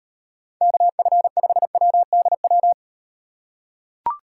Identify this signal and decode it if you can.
Morse code, heard as a keyed tone, about 38 wpm, 700 Hz. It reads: KF5WDW